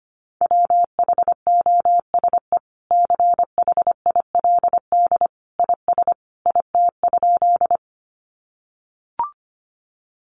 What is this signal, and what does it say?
Morse code, heard as a keyed tone, about 25 wpm, 700 Hz. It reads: WHOSE CHILD IS IT?
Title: Sentences from Top 1200 Words 25wpm